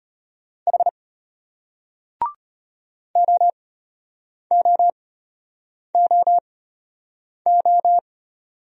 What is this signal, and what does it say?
Morse code, heard as a keyed tone, about 34 wpm, spaced as for 8 wpm, 700 Hz. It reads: H OOOO